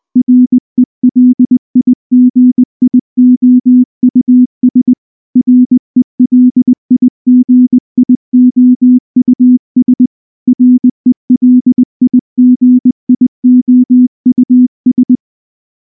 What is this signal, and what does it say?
Morse code, heard as a keyed tone, about 19 wpm, 250 Hz. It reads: RELIGIOUS RELIGIOUS RELIGIOUS
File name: morse / morse-training-20WPM-1